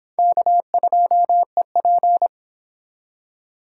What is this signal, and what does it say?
Morse code, heard as a keyed tone, about 30 wpm, 700 Hz. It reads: K2EP